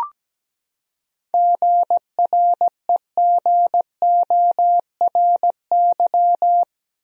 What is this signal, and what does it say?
Morse code, heard as a keyed tone, about 17 wpm, 700 Hz. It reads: GREGORY
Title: USA Common Names - 17wpm